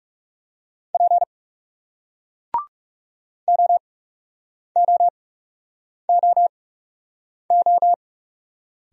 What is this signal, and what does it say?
Morse code, heard as a keyed tone, about 40 wpm, spaced as for 8 wpm, 700 Hz. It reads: P OOOO